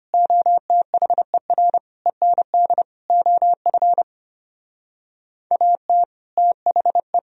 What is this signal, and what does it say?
Morse code, heard as a keyed tone, about 33 wpm, 700 Hz. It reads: OTHER END OF AT THE